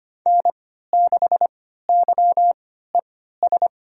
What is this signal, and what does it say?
Morse code, heard as a keyed tone, about 25 wpm, spaced as for 14 wpm, 700 Hz. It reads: N6YES
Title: USA Call Signs - 5 Characters - 3x Character Spacing - 25wpm